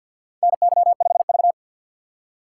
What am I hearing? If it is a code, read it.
Morse code, heard as a keyed tone, about 48 wpm, 700 Hz. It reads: NY5V
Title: USA Call Signs - 4x Character Spacing - 50wpm